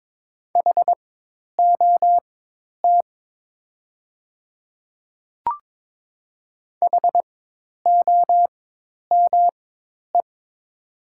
Morse code, heard as a keyed tone, about 22 wpm, spaced as for 10 wpm, 700 Hz. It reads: HOT HOME
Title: 4x Character Spacing - Top 200 Words - 22wpm